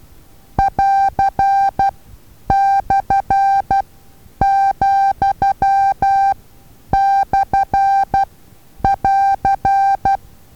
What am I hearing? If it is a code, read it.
Morse code, heard as a keyed tone, about 12 wpm, spaced as for 9 wpm, 790 Hz. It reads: <AR>/,/<AR>
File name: Taviro HA6VA A